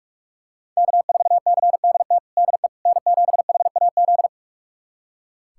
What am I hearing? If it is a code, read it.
Morse code, heard as a keyed tone, about 46 wpm, 700 Hz. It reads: K4CDT DE N7HAZ
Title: USA Callsign DE Callsign - 5 Characters - 45wpm